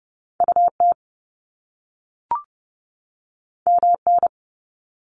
Morse code, heard as a keyed tone, about 30 wpm, 700 Hz. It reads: UT MN
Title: US States 30wpm